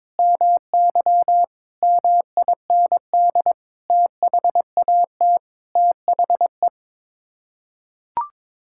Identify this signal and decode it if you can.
Morse code, heard as a keyed tone, about 22 wpm, 700 Hz. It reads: MY MIND THAT THE
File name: Sets of 9 Words - 22wpm